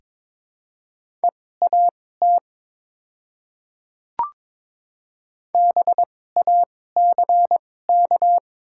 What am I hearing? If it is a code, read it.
Morse code, heard as a keyed tone, about 22 wpm, spaced as for 16 wpm, 700 Hz. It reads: EAT BACK
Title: 2x Character Spacing - Top 300 Words - 22wpm